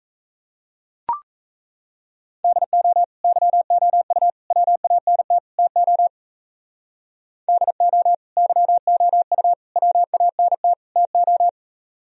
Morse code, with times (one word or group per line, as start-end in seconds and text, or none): 2.4-3.0 DO
3.2-4.3 YOU
4.5-5.4 WANT
5.6-6.1 TO
7.5-8.1 DO
8.4-9.5 YOU
9.8-10.7 WANT
11.0-11.5 TO